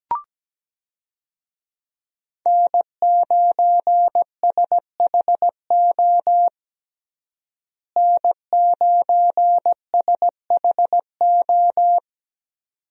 Morse code, heard as a keyed tone, about 17 wpm, 700 Hz. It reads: N9SHO N9SHO
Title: USA Call Signs - 5 Characters - Repeated 2x - 17wpm